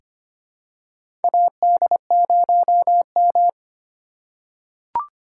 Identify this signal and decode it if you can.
Morse code, heard as a keyed tone, about 25 wpm, 700 Hz. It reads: AD0M
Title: USA Call Signs - 2x Character Spacing - 25wpm